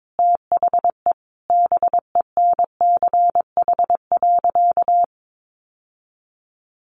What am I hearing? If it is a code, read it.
Morse code, heard as a keyed tone, about 22 wpm, 700 Hz. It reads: THE BENCH.
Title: Sentences from Top 1400 Words 22wpm